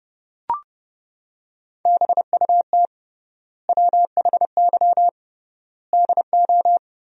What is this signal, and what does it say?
Morse code, heard as a keyed tone, about 32 wpm, 700 Hz. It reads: BUT WHY DO
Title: 3x Word-Spacing - Sentences from Top 200 Words - 30wpm